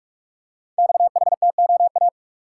Morse code, heard as a keyed tone, about 44 wpm, 700 Hz. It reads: XHTOA
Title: Code Groups - 5 Letters - 45wpm